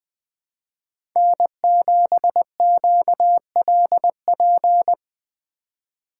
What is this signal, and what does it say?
Morse code, heard as a keyed tone, about 20 wpm, 700 Hz. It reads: N7QLP